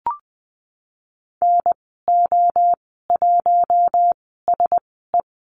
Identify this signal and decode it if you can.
Morse code, heard as a keyed tone, about 19 wpm, spaced as for 14 wpm, 700 Hz. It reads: NO1SE
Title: 069 Lesson 12g - Letter d - Call Signs 20wpm